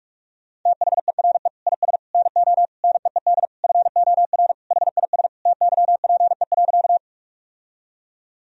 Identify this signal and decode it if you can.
Morse code, heard as a keyed tone, about 47 wpm, 700 Hz. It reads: THERE IS NO NEED FOR HIS TYPE.
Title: Sentences from Top 1000 Words - Review Part 3 - 45wpm